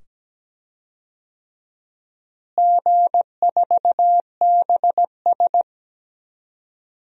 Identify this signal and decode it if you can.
Morse code, heard as a keyed tone, about 17 wpm, 700 Hz. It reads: G4BS
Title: DXCC-100 Call Signs - 4 Characters - Repeated 2x - 17wpm